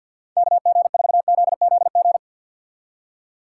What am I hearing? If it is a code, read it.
Morse code, heard as a keyed tone, about 61 wpm, 700 Hz. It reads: KG4ZZG
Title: Speed Racing - USA 2x3 Call Signs 50wpm